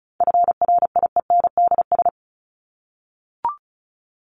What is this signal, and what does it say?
Morse code, heard as a keyed tone, about 34 wpm, 700 Hz. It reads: FRIENDS
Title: ES:S Suffix - Top 100 Words with Base Words 35wpm